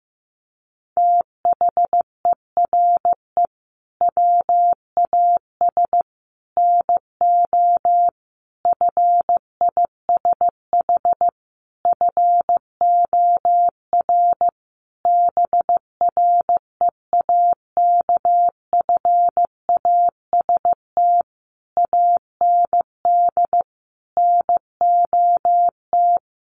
Morse code, heard as a keyed tone, about 15 wpm, 700 Hz. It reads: THERE WAS NO FISH FOR BREAKFAST AND NOT
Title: Sentences from Top 1500 Words 15wpm